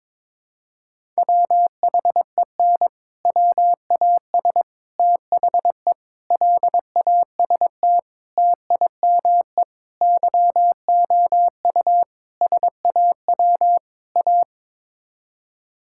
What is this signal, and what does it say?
Morse code, heard as a keyed tone, about 22 wpm, 700 Hz. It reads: WHEN WAS THE LAST TIME YOU SAW A
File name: practice / 22wpm / Sets of 8 Words - 22wpm